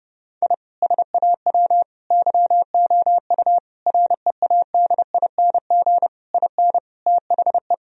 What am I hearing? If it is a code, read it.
Morse code, heard as a keyed tone, about 32 wpm, 700 Hz. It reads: I SAW YOU READING IN THE
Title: Sentences from Top 1000 Words - Review Part 4 - 30wpm